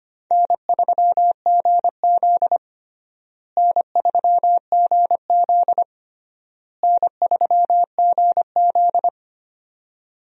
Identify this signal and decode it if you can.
Morse code, heard as a keyed tone, about 25 wpm, 700 Hz. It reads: N3GZ N3GZ N3GZ
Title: USA Call Signs - 4 Characters - Repeated 3x - 25wpm